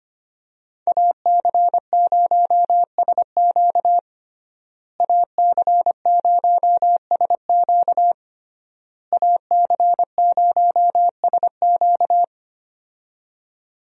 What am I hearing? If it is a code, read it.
Morse code, heard as a keyed tone, about 25 wpm, 700 Hz. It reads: AC0SQ AC0SQ AC0SQ
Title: USA Call Signs - 5 Characters - Repeated 3x - 25wpm